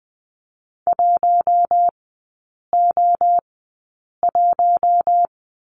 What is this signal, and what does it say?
Morse code, heard as a keyed tone, about 20 wpm, spaced as for 8 wpm, 700 Hz. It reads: 1O1
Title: 047 Lesson 9e - Number 4 - Three Characters 20wpm